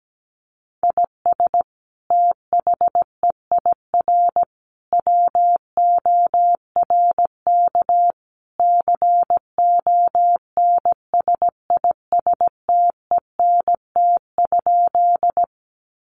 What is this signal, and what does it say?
Morse code, heard as a keyed tone, about 17 wpm, 700 Hz. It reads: IS THEIR WORK CONSISTENT?
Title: Sentences from Top 1500 Words 17wpm